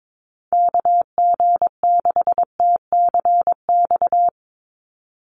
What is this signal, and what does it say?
Morse code, heard as a keyed tone, about 22 wpm, 700 Hz. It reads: KG6TCX
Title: USA Call Signs 22wpm